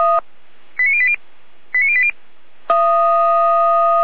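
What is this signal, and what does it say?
Five-tone selective call (ZVEI), heard as quick runs of five tones, tones of about 70 ms. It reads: 89080 89080
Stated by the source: ZVEI five-tone selective call received via SDR.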